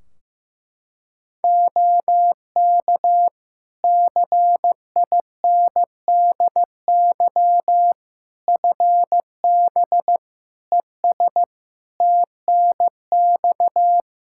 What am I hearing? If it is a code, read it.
Morse code, heard as a keyed tone, about 16 wpm, 700 Hz. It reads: OK CINDY FB ES TNX
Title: K1USN SST QSO Practice 15wpm